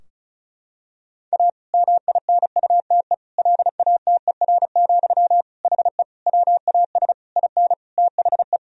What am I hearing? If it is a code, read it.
Morse code, heard as a keyed tone, about 34 wpm, 700 Hz. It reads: A MINUTE LATER, HE WAS IN THE
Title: Sentences from Top 1000 Words - Review Part 4 - 35wpm